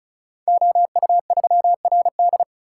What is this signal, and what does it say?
Morse code, heard as a keyed tone, about 36 wpm, 690 Hz. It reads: OU3RD
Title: Code Groups - 5 Letter:Numbers - 35wpm